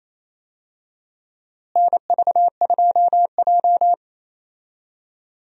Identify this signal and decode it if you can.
Morse code, heard as a keyed tone, about 28 wpm, 700 Hz. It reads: NV2J
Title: USA Call Signs - 4 Characters - Repeated 3x - 28wpm